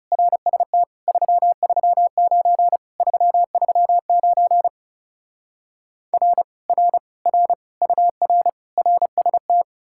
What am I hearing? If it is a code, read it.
Morse code, heard as a keyed tone, about 39 wpm, 700 Hz. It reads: RST 339 339 R R R UR RST